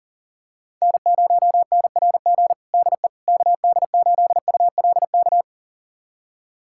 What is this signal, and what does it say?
Morse code, heard as a keyed tone, about 40 wpm, 700 Hz. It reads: N0NRG DE KD8ULK
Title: USA Callsign DE Callsign 40wpm